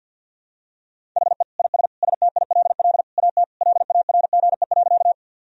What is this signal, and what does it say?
Morse code, heard as a keyed tone, about 51 wpm, 700 Hz. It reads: HE IS STILL AT LARGE.
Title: Sentences from Top 1000 Words - Review Part 1 - 50wpm